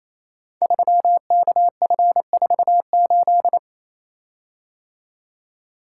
Morse code, heard as a keyed tone, about 28 wpm, 700 Hz. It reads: 3KF48
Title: Code Groups - 5 Letter:Numbers - 28wpm